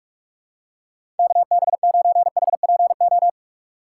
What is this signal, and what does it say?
Morse code, heard as a keyed tone, about 42 wpm, 700 Hz. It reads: KB9HPO
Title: USA Call Signs - 6 Characters - 3x Character Spacing - 45wpm